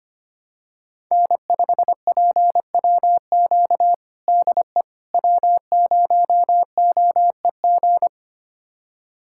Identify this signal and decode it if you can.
Morse code, heard as a keyed tone, about 25 wpm, 700 Hz. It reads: N5PWQ DE W0OEG
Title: USA Callsign DE Callsign - 5 Characters - 25wpm